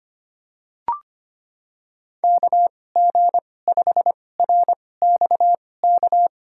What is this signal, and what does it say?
Morse code, heard as a keyed tone, about 25 wpm, spaced as for 18 wpm, 700 Hz. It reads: KG5RXK
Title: USA Call Signs - 6 Characters - 2x Character Spacing - 25wpm